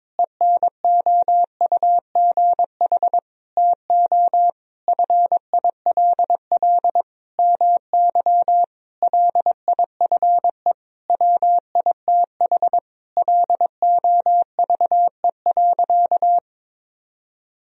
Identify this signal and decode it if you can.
Morse code, heard as a keyed tone, about 22 wpm, 700 Hz. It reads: ENOUGH TO FILL MY LIFE WITH LOVE.
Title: Sentences from Top 1000 Words - Review Part 3 - 22wpm